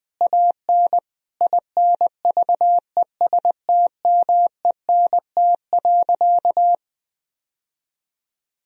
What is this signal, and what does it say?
Morse code, heard as a keyed tone, about 21 wpm, 700 Hz. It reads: AN INVESTMENT.
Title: Sentences from Top 1000 Words - Review Part 4 - 20wpm